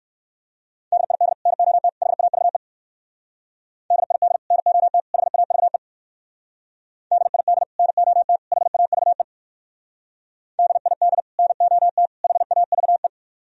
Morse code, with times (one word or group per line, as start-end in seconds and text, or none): 0.9-1.3 DID
1.5-1.9 NOT
2.0-2.6 HAVE
3.9-4.4 DID
4.5-5.0 NOT
5.1-5.8 HAVE
7.1-7.6 DID
7.8-8.4 NOT
8.5-9.2 HAVE
10.6-11.2 DID
11.4-12.1 NOT
12.2-13.1 HAVE